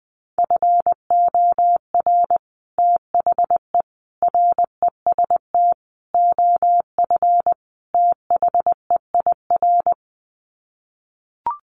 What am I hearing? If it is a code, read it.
Morse code, heard as a keyed tone, about 20 wpm, 700 Hz. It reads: FOR THE REST OF THEIR
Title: Sets of 5 Words Encore2 20wpm